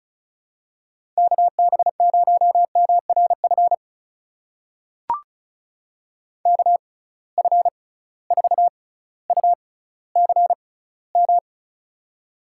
Morse code, heard as a keyed tone, about 36 wpm, spaced as for 18 wpm, 700 Hz. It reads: KB0MRF KF4UCM
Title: USA Call Signs - 6 Characters - 6x Character Spacing - 35wpm